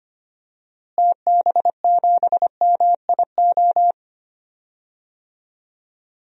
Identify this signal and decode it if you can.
Morse code, heard as a keyed tone, about 25 wpm, 700 Hz. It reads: TB7MIO
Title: DXCC-100 Call Signs - 6 Characters - Repeated 2x - 25wpm